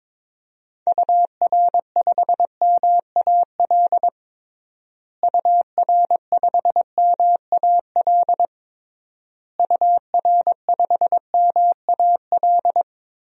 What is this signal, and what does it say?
Morse code, heard as a keyed tone, about 21 wpm, 700 Hz. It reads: UR5MAL UR5MAL UR5MAL
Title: DXCC-100 Call Signs - 6 Characters - Repeated 3x - 22wpm